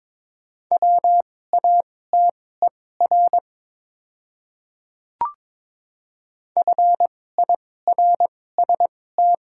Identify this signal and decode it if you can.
Morse code, heard as a keyed tone, about 22 wpm, spaced as for 16 wpm, 700 Hz. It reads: WATER FIRST
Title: 2x Character Spacing - Top 300 Words - 22wpm